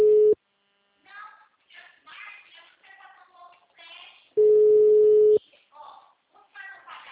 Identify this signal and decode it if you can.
Ringback tone (Europe, 425 Hz), with speech.
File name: out- -2012- - - .50357